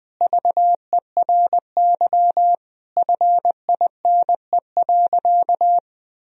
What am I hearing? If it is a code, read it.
Morse code, heard as a keyed tone, about 21 wpm, 700 Hz. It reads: VERY FINE.
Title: Sentences from Top 1000 Words - Review Part 1 - 20wpm